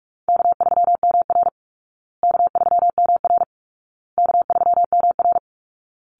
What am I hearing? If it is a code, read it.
Morse code, heard as a keyed tone, about 44 wpm, 700 Hz. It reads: K3MR K3MR K3MR
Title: Easy Call Signs Repeated 3x - 45wpm